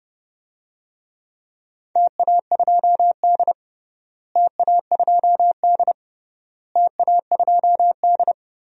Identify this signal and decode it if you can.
Morse code, heard as a keyed tone, about 30 wpm, 700 Hz. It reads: TA2D TA2D TA2D
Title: DXCC-100 Call Signs - 4 Characters - Repeated 3x - 30wpm